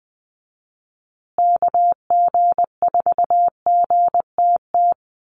Morse code, heard as a keyed tone, about 20 wpm, 700 Hz. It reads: KG4GTT
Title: USA Call Signs Repeated 2x 20wpm